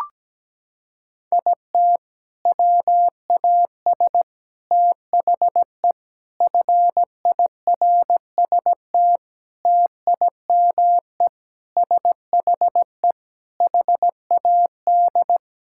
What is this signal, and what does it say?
Morse code, heard as a keyed tone, about 17 wpm, 700 Hz. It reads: IT WAS THE FIRST TIME SHE HAD
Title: Sentences from Top 1000 Words - Review Part 6 - 17wpm